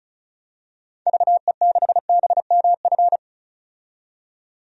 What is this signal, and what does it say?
Morse code, heard as a keyed tone, about 35 wpm, 700 Hz. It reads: VE6BMF
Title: DXCC-100 Call Signs - 6 Characters - 35wpm